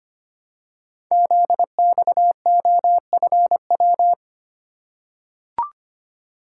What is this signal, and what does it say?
Morse code, heard as a keyed tone, about 25 wpm, 700 Hz. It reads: ZXOFW
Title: Code Groups - 5 Letter:Numbers - 25wpm